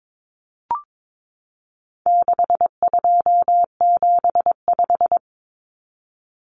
Morse code, heard as a keyed tone, about 22 wpm, 700 Hz. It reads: 6275
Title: Numbers - 4-Digit 22wpm